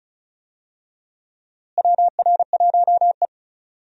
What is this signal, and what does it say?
Morse code, heard as a keyed tone, about 34 wpm, 700 Hz. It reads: WR1E
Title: USA Call Signs - 4 Characters - Repeated 3x - 35wpm